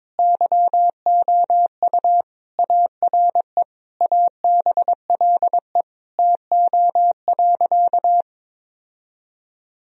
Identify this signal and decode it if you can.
Morse code, heard as a keyed tone, about 22 wpm, 700 Hz. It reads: YOU ARE ABLE TO.
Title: Sentences from Top 1000 Words - Review Part 2 - 22wpm